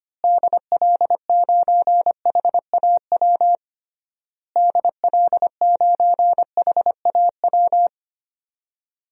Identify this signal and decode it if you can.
Morse code, heard as a keyed tone, about 25 wpm, 700 Hz. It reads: DL9HAW DL9HAW
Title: DXCC-100 Call Signs - 6 Characters - Repeated 3x - 25wpm